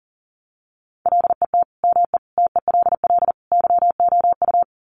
Morse code, heard as a keyed tone, about 40 wpm, 700 Hz. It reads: LET ME TELL YOU
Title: Sets of 4 Words 40wpm